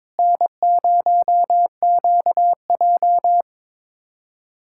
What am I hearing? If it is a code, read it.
Morse code, heard as a keyed tone, about 22 wpm, 700 Hz. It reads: N0QJ
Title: USA Callsign DE Callsign - 4 Characters - 22wpm